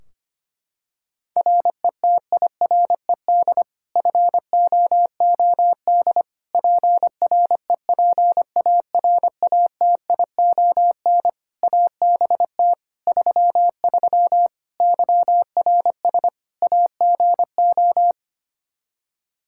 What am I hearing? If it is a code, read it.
Morse code, heard as a keyed tone, about 25 wpm, 700 Hz. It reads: RETIRED FOOD PREPARATION ABT 33 YRS AGO
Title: K1USN SST QSO Practice 25wpm